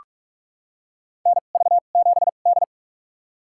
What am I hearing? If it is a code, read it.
Morse code, heard as a keyed tone, about 47 wpm, spaced as for 33 wpm, 700 Hz. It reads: NV7D
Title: USA Call Signs - 4 Characters - 2x Character Spacing - 45wpm